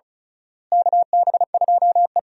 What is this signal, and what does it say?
Morse code, heard as a keyed tone, about 34 wpm, 700 Hz. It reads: KB2E
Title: USA Call Signs - 4 Characters - Repeated 3x - 35wpm